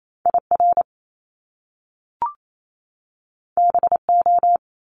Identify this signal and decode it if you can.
Morse code, heard as a keyed tone, about 28 wpm, 700 Hz. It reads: IR BO
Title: Top Two Letter Combinations in Words 28wpm